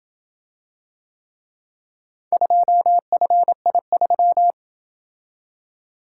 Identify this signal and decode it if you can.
Morse code, heard as a keyed tone, about 27 wpm, 700 Hz. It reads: 2FI3